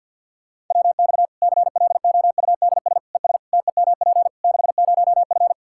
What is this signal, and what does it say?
Morse code, heard as a keyed tone, about 51 wpm, 700 Hz. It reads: WX CLOUDS ES TEMP 60F
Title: Practice QSO - Exchange 2 - Info - 50wpm